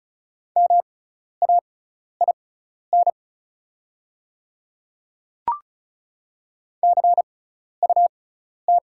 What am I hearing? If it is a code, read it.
Morse code, heard as a keyed tone, about 36 wpm, spaced as for 12 wpm, 690 Hz. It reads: MAIN CUT
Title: 6x Character Spacing - Top 300 Words - 35wpm